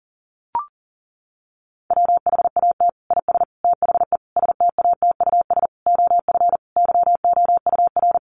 Morse code, heard as a keyed tone, about 40 wpm, 700 Hz. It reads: WHAT IS THE STATUS OF YOUR
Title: Sentences from Top 1400 Words 40wpm